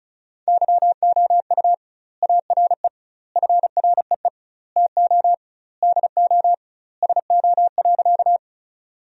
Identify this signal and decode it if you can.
Morse code, heard as a keyed tone, about 34 wpm, 700 Hz. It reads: YOU ARE FREE TO DO SO.
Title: 2x Word-Spacing - Sentences from Top 500 Words - 35wpm